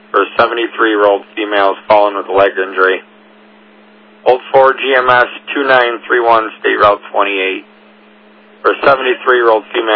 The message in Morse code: none

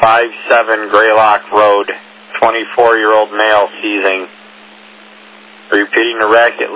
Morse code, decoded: none

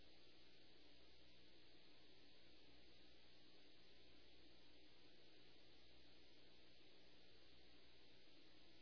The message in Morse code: none

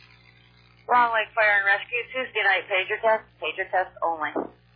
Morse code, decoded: none

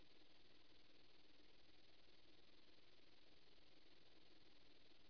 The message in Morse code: none

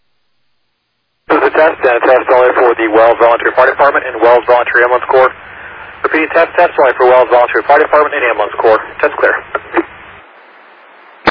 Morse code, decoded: none

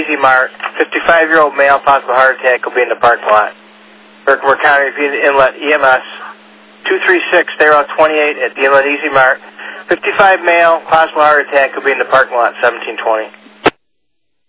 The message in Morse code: none

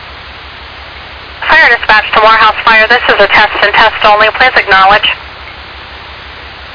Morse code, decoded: none

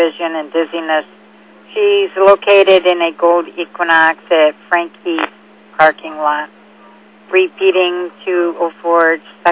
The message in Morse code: none

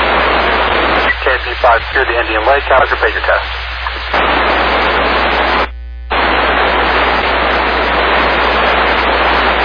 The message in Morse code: none

